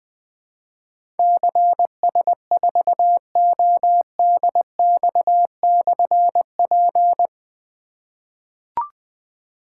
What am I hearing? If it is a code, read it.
Morse code, heard as a keyed tone, about 20 wpm, 700 Hz. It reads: CS4ODX/P
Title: Difficult Call Signs 20wpm